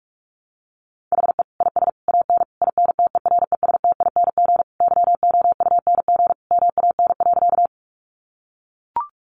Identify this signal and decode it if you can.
Morse code, heard as a keyed tone, about 44 wpm, 700 Hz. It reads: HE IS AN INTERESTING YOUNG MAN.
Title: Sentences from Top 1000 Words 45wpm